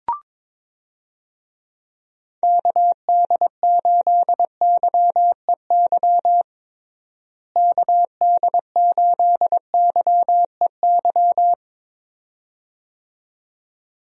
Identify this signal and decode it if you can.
Morse code, heard as a keyed tone, about 22 wpm, 700 Hz. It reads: KD8YEY KD8YEY
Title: USA Call Signs - 6 Characters - Repeated 2x - 22wpm